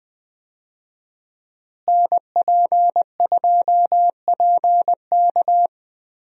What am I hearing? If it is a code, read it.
Morse code, heard as a keyed tone, about 20 wpm, 700 Hz. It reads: NP2PK
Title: USA Call Signs - 5 Characters - Repeated 2x - 20wpm